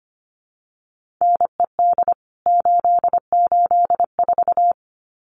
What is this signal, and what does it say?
Morse code, heard as a keyed tone, about 25 wpm, 700 Hz. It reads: NED 884
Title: CWT Exchange 25wpm